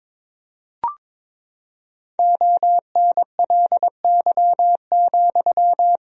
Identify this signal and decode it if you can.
Morse code, heard as a keyed tone, about 22 wpm, 700 Hz. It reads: ONLY,
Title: 4x Word-Spacing - Sentences from Top 300 Words - 22wpm